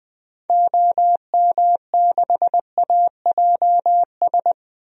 Morse code, heard as a keyed tone, about 20 wpm, 700 Hz. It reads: OM6AJS
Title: DXCC-100 Call Signs - 6 Characters - Repeated 2x - 20wpm